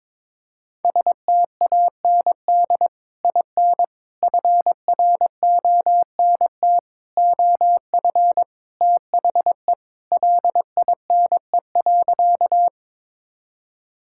Morse code, heard as a keyed tone, about 22 wpm, 700 Hz. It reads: STAND IN FRONT OF THE LINE.
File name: practice / 22wpm / Sentences with 3-Word Phrases - 22wpm